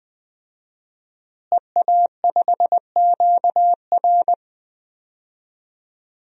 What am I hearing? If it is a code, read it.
Morse code, heard as a keyed tone, about 20 wpm, 700 Hz. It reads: EA5QR
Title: DXCC-100 Call Signs - 5 Characters - Repeated 2x - 20wpm